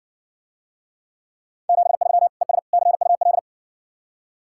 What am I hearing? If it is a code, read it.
Morse code, heard as a keyed tone, about 56 wpm, 700 Hz. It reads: 73 ES CUL